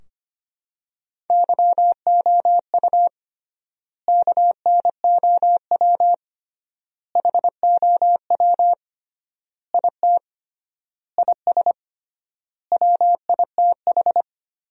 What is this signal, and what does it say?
Morse code, heard as a keyed tone, about 25 wpm, 700 Hz. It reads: YOU KNOW HOW IT IS WITH
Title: 3x Word-Spacing - Sentences from Top 100 Words - 25wpm